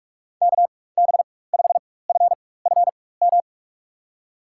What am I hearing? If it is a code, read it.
Morse code, heard as a keyed tone, about 47 wpm, spaced as for 21 wpm, 700 Hz. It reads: KB5FFM
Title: USA Call Signs - 6 Characters - 4x Character Spacing - 45wpm